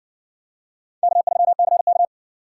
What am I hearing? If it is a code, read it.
Morse code, heard as a keyed tone, about 60 wpm, 700 Hz. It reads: K3CK